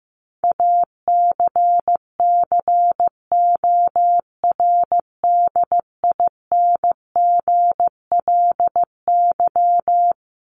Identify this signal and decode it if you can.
Morse code, heard as a keyed tone, about 15 wpm, 700 Hz. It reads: ACCORDINGLY
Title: LY Suffix - Top 100 Words with Base Words 15wpm